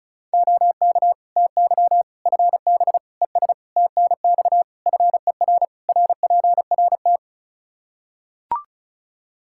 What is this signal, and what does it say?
Morse code, heard as a keyed tone, about 34 wpm, 700 Hz. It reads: OK TY FB ES TNX FER RPRT
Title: Practice QSO - Exchange 2 - Info - 35wpm